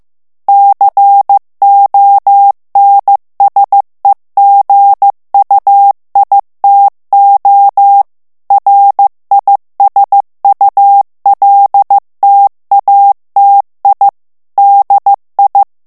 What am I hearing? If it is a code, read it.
Morse code, heard as a keyed tone, about 15 wpm, 800 Hz. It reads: CONSEGUITO RISULTATI DI
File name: morse mp3 / ITALIEN texte clair 2 vitesse 14 mots minute